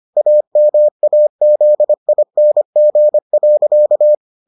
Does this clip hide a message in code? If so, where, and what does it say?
Yes: Morse code from 0.2 to 4.1 s, heard as a keyed tone, about 25 wpm, 600 Hz. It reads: AMAZING.